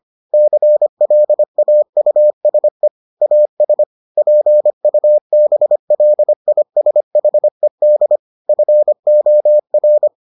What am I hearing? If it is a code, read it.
Morse code, heard as a keyed tone, about 25 wpm, 600 Hz. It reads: CLAUSE AS PUBLISHED FOR